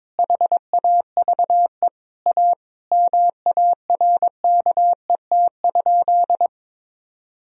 Morse code, heard as a keyed tone, about 22 wpm, 700 Hz. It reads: HAVE A MARKET?
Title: Sentences from Top 1000 Words - Review Part 3 - 22wpm